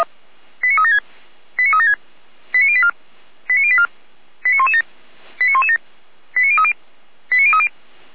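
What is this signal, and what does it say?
Five-tone selective call (ZVEI), heard as quick runs of five tones, tones of about 70 ms. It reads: 89376 89376 89084 89084 89117 89117 89030 89030
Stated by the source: ZVEI five-tone selective call 89080 received on SDR.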